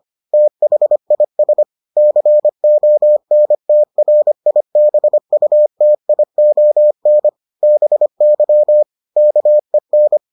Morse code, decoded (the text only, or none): THIS CONTRIBUTION BY KEN